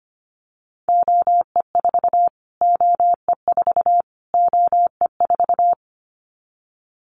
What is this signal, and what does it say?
Morse code, heard as a keyed tone, about 25 wpm, 700 Hz. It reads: OE4 OE4 OE4
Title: Callsign Prefixes for Top 25 DXCC - Repeated 3x - 25wpm